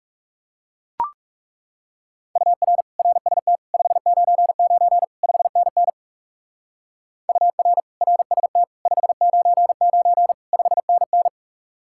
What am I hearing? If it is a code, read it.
Morse code, heard as a keyed tone, about 36 wpm, 700 Hz. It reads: UR RST 599 5NN UR RST 599 5NN
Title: Speed-Racing Common QSO Phrases - 30wpm